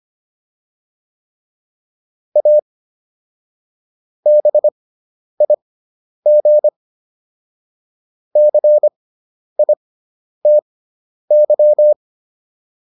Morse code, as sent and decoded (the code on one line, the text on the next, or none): .- / -... .. --. / -.-. .. - -.--
A BIG CITY